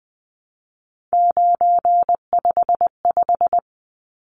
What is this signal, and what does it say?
Morse code, heard as a keyed tone, about 20 wpm, 700 Hz. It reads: 955
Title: Numbers - 3-Digit 20wpm